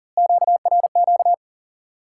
Morse code, heard as a keyed tone, about 40 wpm, 700 Hz. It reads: QRQ
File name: IQE Course 3 - Lesson 89 - YAGI - Two Elements - 40wpm